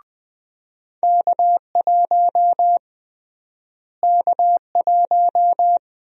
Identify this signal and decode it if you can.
Morse code, heard as a keyed tone, about 21 wpm, 700 Hz. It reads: K1 K1
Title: USA Call Sign Prefixes - Repeated 2x - 20wpm